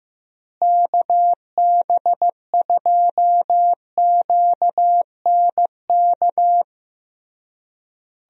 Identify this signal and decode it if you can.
Morse code, heard as a keyed tone, about 15 wpm, 700 Hz. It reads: KB2QNK
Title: USA Call Signs - 6 Characters - Repeated 3x - 15wpm